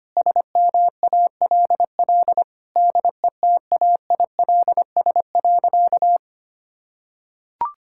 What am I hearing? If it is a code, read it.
Morse code, heard as a keyed tone, about 25 wpm, 700 Hz. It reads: SMALL DETAILS.
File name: practice / 25wpm / Sentences with 9-Word Phrases - 25wpm